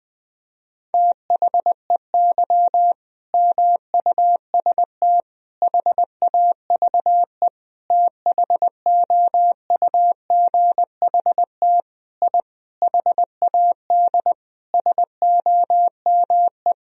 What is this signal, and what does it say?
Morse code, heard as a keyed tone, about 20 wpm, 700 Hz. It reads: THEY MUST HAVE THOUGHT I HAD SOME
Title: Sentences from Top 1000 Words - Review Part 4 - 20wpm